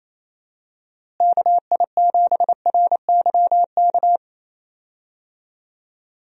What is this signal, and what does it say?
Morse code, heard as a keyed tone, about 28 wpm, 700 Hz. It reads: KI7RYK
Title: USA Call Signs - 6 Characters - Repeated 2x - 28wpm